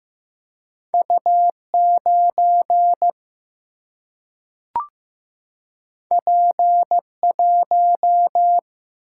Morse code, heard as a keyed tone, about 16 wpm, 700 Hz. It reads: U9 P1
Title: Single-Letter Single-Number 15wpm